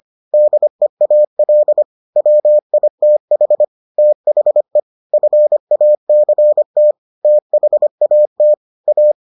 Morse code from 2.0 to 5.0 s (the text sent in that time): WITH THE